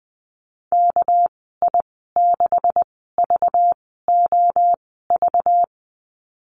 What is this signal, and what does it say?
Morse code, heard as a keyed tone, about 20 wpm, spaced as for 14 wpm, 700 Hz. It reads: KI6VOV
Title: 207 Lesson 33g - Letter j - Call Signs 20wpm